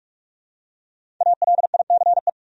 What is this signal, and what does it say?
Morse code, heard as a keyed tone, about 43 wpm, 700 Hz. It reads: ALICE
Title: USA Common Names - Max 5 Letters - 45wpm